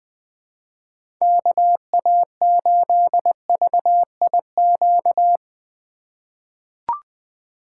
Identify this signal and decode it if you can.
Morse code, heard as a keyed tone, about 20 wpm, 700 Hz. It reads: KA8VIQ
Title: USA Call Signs - 6 Characters - 2x Character Spacing - 20wpm